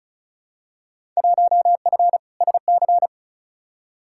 Morse code, heard as a keyed tone, about 36 wpm, 700 Hz. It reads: 1F SC
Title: Field Day Exchanges Encore - 35wpm